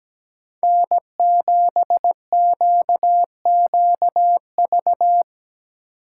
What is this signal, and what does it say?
Morse code, heard as a keyed tone, about 17 wpm, 700 Hz. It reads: N7QQV
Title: USA Call Signs - 5 Characters - 2x Character Spacing - 17wpm